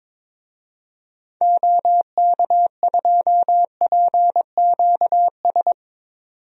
Morse code, heard as a keyed tone, about 22 wpm, 700 Hz. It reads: OK2PQS
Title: DXCC-100 Call Signs - 6 Characters - 22wpm